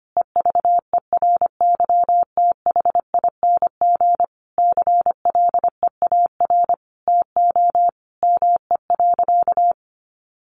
Morse code, heard as a keyed tone, about 25 wpm, 700 Hz. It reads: EVERYTHING CLEAR TO ME.
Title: Sentences from Top 1300 Words 25wpm